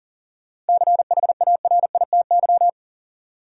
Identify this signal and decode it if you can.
Morse code, heard as a keyed tone, about 40 wpm, 700 Hz. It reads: CHARITY
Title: Seven Letter Words - 40wpm